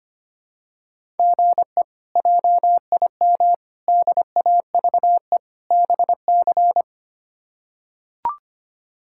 Morse code, heard as a keyed tone, about 25 wpm, 700 Hz. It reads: GE JIM DAVE BC
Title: K1USN SST QSO Practice - 25wpm